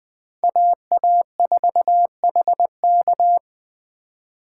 Morse code, heard as a keyed tone, about 20 wpm, 700 Hz. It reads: AA4HK